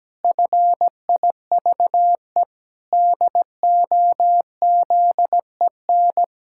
Morse code, heard as a keyed tone, about 17 wpm, 700 Hz. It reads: FIVE DOZEN
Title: Pangram Challenge v2 - 17wpm